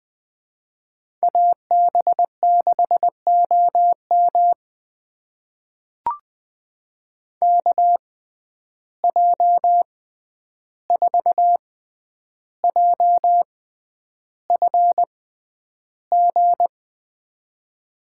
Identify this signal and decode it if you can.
Morse code, heard as a keyed tone, about 20 wpm, spaced as for 10 wpm, 700 Hz. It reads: AB6OM KJ4JFG
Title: USA Call Signs - 6x Character Spacing - 20wpm